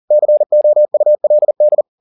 Morse code, heard as a keyed tone, about 40 wpm, 600 Hz. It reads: COULD
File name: the old man and the sea 2 40wpm